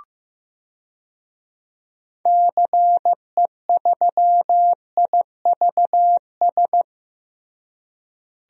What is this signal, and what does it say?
Morse code, heard as a keyed tone, about 15 wpm, 700 Hz. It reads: CE3IVS
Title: DXCC-100 Call Signs - 6 Characters - Repeated 2x - 15wpm